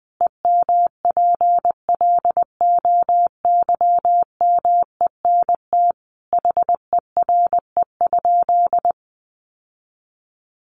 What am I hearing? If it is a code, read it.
Morse code, heard as a keyed tone, about 20 wpm, 700 Hz. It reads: EMPLOYMENT HERE?
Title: Sentences from Top 1500 Words 20wpm